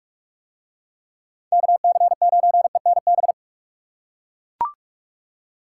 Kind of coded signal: Morse code

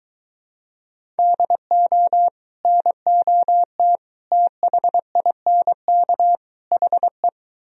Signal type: Morse code

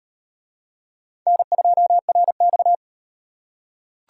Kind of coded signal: Morse code

